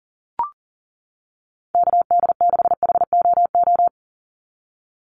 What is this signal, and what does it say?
Morse code, heard as a keyed tone, about 39 wpm, 690 Hz. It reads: KD6HOO
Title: USA Call Signs 40wpm